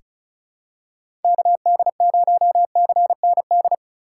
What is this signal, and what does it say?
Morse code, heard as a keyed tone, about 36 wpm, 700 Hz. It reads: KD0CND
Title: USA Call Signs - 6x Character Spacing - 35wpm